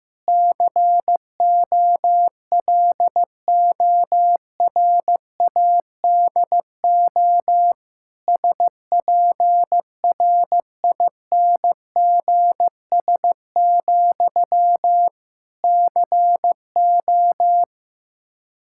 Morse code, heard as a keyed tone, about 15 wpm, 700 Hz. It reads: COLORADO SPRINGS, CO
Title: Practice QSO - Exchange 1 - Beginning - 15wpm